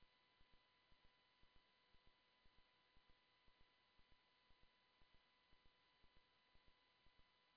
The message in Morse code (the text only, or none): none